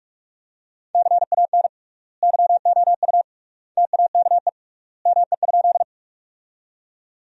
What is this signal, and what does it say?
Morse code, heard as a keyed tone, about 42 wpm, 700 Hz. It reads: CAN YOU TAKE ME?
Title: 3x Word-Spacing - Sentences from Top 200 Words - 45wpm